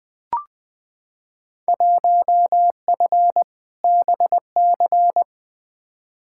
Morse code, heard as a keyed tone, about 20 wpm, 700 Hz. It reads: 1F BC
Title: Field Day Exchanges Encore - 20wpm